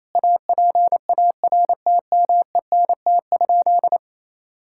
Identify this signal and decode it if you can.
Morse code, heard as a keyed tone, about 28 wpm, 700 Hz. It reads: APARTMENT?
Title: Sentences from Top 1000 Words - Review Part 6 - 28wpm